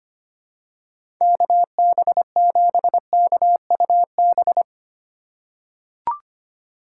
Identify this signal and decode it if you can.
Morse code, heard as a keyed tone, about 25 wpm, 700 Hz. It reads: KB7KUB
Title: USA Call Signs - 6 Characters - 2x Character Spacing - 25wpm